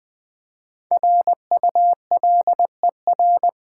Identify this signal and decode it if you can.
Morse code, heard as a keyed tone, about 20 wpm, 700 Hz. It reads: RULER
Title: Easy Words with Hints - 20wpm